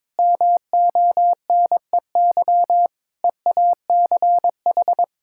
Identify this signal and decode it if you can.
Morse code, heard as a keyed tone, about 21 wpm, 700 Hz. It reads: MONEY EACH
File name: Difficult Sentences with Hints - 22wpm